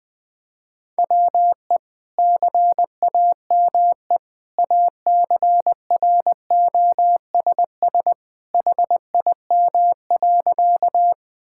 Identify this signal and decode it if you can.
Morse code, heard as a keyed tone, about 21 wpm, 700 Hz. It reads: WE CAME ACROSS HIM.
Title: Sentences from Top 1000 Words - Review Part 5 - 20wpm